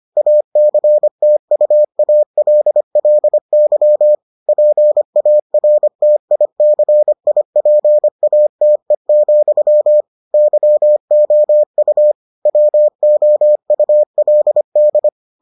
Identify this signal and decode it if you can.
Morse code, heard as a keyed tone, about 25 wpm, 600 Hz. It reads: ACTUALLY PARTICIPATE, YOU WOULD